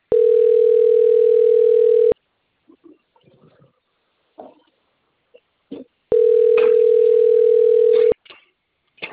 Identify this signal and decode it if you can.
Ringback tone (North America), 440 + 480 Hz.
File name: out- - - - - .58088